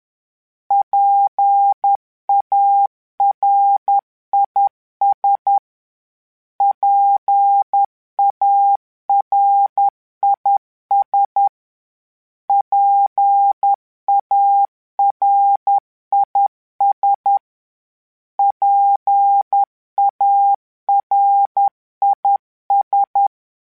Morse code, heard as a keyed tone, about 10 wpm, 800 Hz. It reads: PARIS PARIS PARIS PARIS
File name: PARIS Calibration / PARIS mp3s / PARIS@10